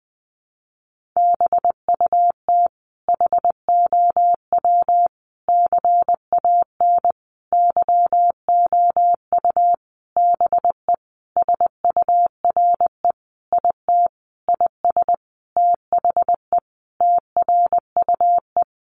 Morse code, heard as a keyed tone, about 20 wpm, 700 Hz. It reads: BUT HOW CAN YOU BE SURE IT IS THE TRUE
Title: Sentences from Top 1400 Words 20wpm